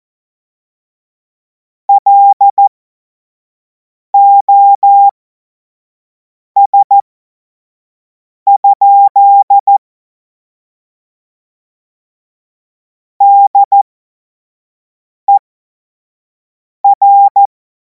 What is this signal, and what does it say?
Morse code, heard as a keyed tone, about 14 wpm, spaced as for 5 wpm, 800 Hz. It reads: LOS? DER
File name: Morsekurs / Witze / Witz 07. Juli 2026 25 F70